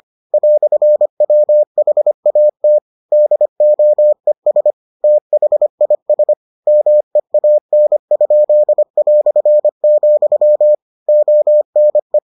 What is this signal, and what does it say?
Morse code, heard as a keyed tone, about 25 wpm, 600 Hz. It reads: "WHAT DOES THIS MEAN?", ONE